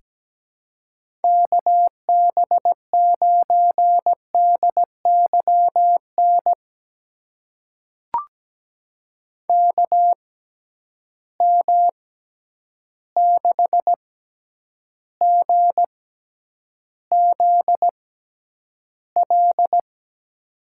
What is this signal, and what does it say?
Morse code, heard as a keyed tone, about 17 wpm, spaced as for 9 wpm, 700 Hz. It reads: KB9DYN KM6GZL
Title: USA Call Signs - 6 Characters - 6x Character Spacing - 17wpm